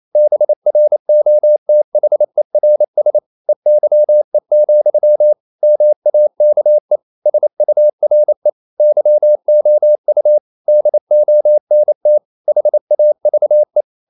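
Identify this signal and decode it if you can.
Morse code, heard as a keyed tone, about 28 wpm, 600 Hz. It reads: BROTHERS EYE, MAKE SURE YOU DONT HAVE
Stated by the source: Morse code practice at 28 wpm, text of Ken Follett's novel Never in English, from CW med Gnister.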